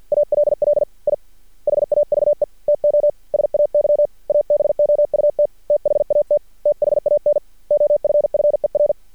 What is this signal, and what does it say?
Morse code, heard as a keyed tone, about 48 wpm, 590 Hz. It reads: ALL I HAVE TO SAY ABOUT THAT THAN OFFER
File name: archived / CRSnet